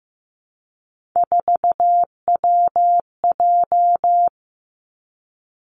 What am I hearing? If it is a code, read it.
Morse code, heard as a keyed tone, about 15 wpm, 700 Hz. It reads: 4WJ
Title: Three Challenging Letter-Number Combinations 15wpm